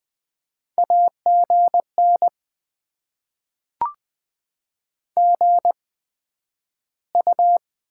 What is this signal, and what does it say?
Morse code, heard as a keyed tone, about 20 wpm, spaced as for 11 wpm, 700 Hz. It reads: AGN GU
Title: Top 20 QSO Elements - 8x Character Spacing - 20wpm